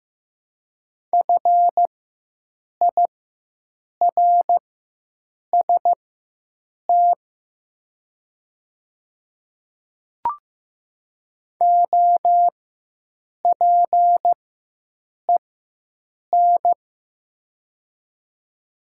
Morse code, heard as a keyed tone, about 15 wpm, spaced as for 7 wpm, 700 Hz. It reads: FIRST OPEN
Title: 4x Character Spacing - Top 300 Words - 15wpm